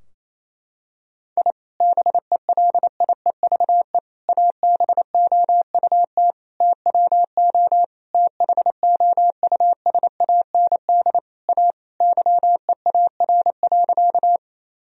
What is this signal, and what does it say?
Morse code, heard as a keyed tone, about 27 wpm, 700 Hz. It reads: I BELIEVE ABOUT TWO THOUSAND A YEAR.
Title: Sentences from Top 1000 Words - Review Part 3 - 28wpm